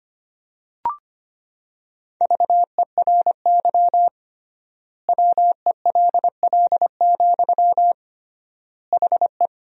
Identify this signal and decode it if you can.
Morse code, heard as a keyed tone, about 25 wpm, 700 Hz. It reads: VERY WELL, HE
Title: 3x Word-Spacing - Sentences from Top 200 Words - 25wpm